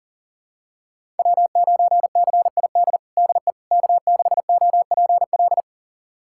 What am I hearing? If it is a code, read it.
Morse code, heard as a keyed tone, about 37 wpm, 700 Hz. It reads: W9CID DE K6OPL
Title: USA Callsign DE Callsign - 5 Characters - 40wpm